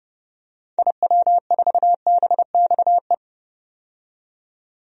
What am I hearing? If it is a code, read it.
Morse code, heard as a keyed tone, about 32 wpm, 700 Hz. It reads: IW4BXE
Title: DXCC-100 Call Signs - 6 Characters - 30wpm